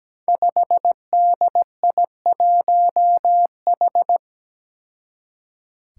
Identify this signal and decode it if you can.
Morse code, heard as a keyed tone, about 17 wpm, 700 Hz. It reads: 5DI1H
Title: Five Challenging Letter-Number Combinations 17wpm